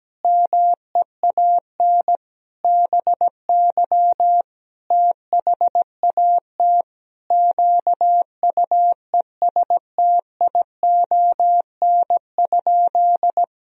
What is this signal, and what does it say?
Morse code, heard as a keyed tone, about 17 wpm, 700 Hz. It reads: MEAN BY THAT QUESTION?
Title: Sentences from Top 400 Words and Abbreviations 17wpm